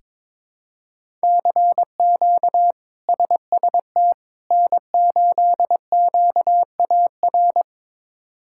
Morse code, heard as a keyed tone, about 22 wpm, 700 Hz. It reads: CQ SST N8QAR
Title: K1USN SST QSO Practice - 22wpm